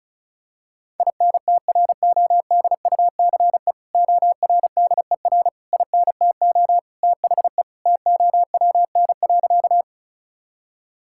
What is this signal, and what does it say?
Morse code, heard as a keyed tone, about 36 wpm, 700 Hz. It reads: INTRODUCE ORDER INTO THE TOWN.
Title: Sentences from Top 1600 Words 35wpm